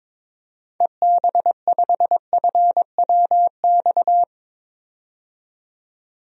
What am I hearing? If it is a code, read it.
Morse code, heard as a keyed tone, about 22 wpm, 700 Hz. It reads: EB5FWX
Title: DXCC-100 Call Signs - 6 Characters - 22wpm